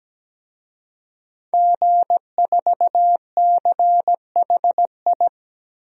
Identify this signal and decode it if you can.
Morse code, heard as a keyed tone, about 17 wpm, 700 Hz. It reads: G4CHI
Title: DXCC-100 Call Signs - 5 Characters - Repeated 2x - 17wpm